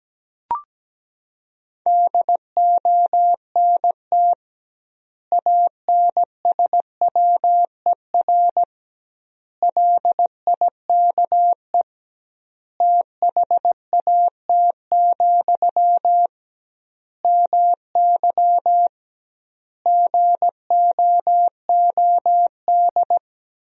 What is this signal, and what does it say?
Morse code, heard as a keyed tone, about 17 wpm, 700 Hz. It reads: DONT ANSWER LIKE THAT, MY GOOD
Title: 2x Word-Spacing - Sentences from Top 400 Words - 17wpm